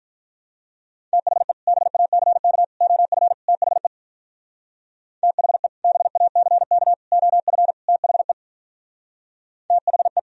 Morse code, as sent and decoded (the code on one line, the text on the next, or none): - .... / -... .- -.-. -.- / --- ..-. / - .... / - .... / -... .- -.-. -.- / --- ..-. / - .... / - .... .
THE BACK OF THE THE BACK OF THE THE